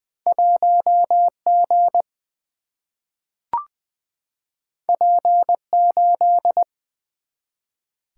Morse code, heard as a keyed tone, about 21 wpm, 700 Hz. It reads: 1G P8
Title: Single-Letter Single-Number 20wpm